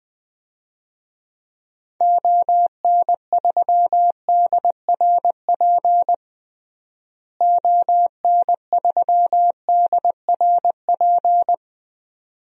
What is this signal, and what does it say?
Morse code, heard as a keyed tone, about 20 wpm, 700 Hz. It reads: ON3DRP ON3DRP
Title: DXCC-100 Call Signs - 6 Characters - Repeated 2x - 20wpm